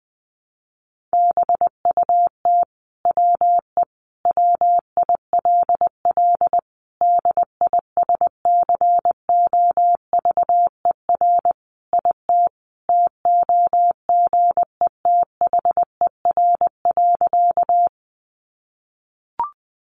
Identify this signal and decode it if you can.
Morse code, heard as a keyed tone, about 20 wpm, 700 Hz. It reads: BUT WE WILL DISCOVER IT TOGETHER.
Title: Sentences from Top 1100 Words 20wpm